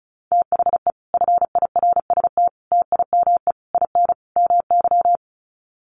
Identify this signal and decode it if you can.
Morse code, heard as a keyed tone, about 34 wpm, 700 Hz. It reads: THE FIRST TIME IN MY
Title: Sets of 5 Words Encore 35wpm